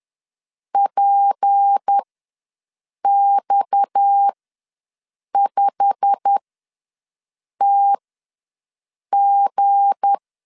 Morse code, heard as a keyed tone, about 10 wpm, spaced as for 6 wpm, 800 Hz. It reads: PX5TG
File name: LEZIONI / 9 286 ws 10wpm 4032